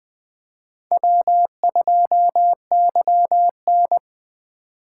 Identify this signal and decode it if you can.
Morse code, heard as a keyed tone, about 20 wpm, 700 Hz. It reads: W2YN
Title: Speed Racing - USA 1x2 Call Signs 17wpm